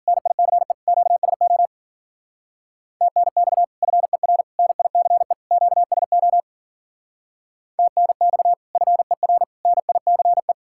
Morse code, heard as a keyed tone, about 52 wpm, 700 Hz. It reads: NICE QSO TNX FER NICE QSO TNX FER NICE